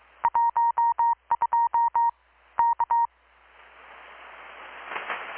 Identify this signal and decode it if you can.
Morse code, heard as a keyed tone, about 24 wpm, 940 Hz. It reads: 12 K
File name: blog / swl / 6-abril-2019-12695-USB-KPS